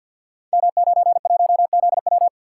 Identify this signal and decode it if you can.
Morse code, heard as a keyed tone, about 51 wpm, 700 Hz. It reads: M91ZW
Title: Code Groups - 5 Letter:Numbers - 50wpm